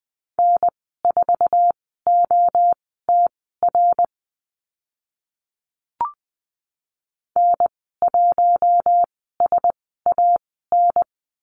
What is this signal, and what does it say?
Morse code, heard as a keyed tone, about 20 wpm, spaced as for 14 wpm, 700 Hz. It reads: N4OTR N1SAN
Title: 055 Lesson 10g - Letter r - Call Signs 20wpm